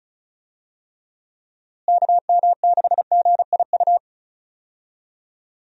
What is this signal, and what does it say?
Morse code, heard as a keyed tone, about 34 wpm, 700 Hz. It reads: KM6GIU
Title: USA Call Signs - 6 Characters - Repeated 2x - 35wpm